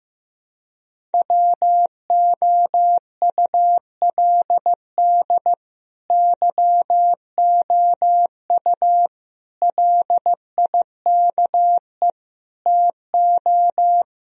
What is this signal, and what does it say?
Morse code, heard as a keyed tone, about 15 wpm, 700 Hz. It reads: WOULD YOU LIKE TO
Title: Sentences from Top 1000 Words - Review Part 1 - 15wpm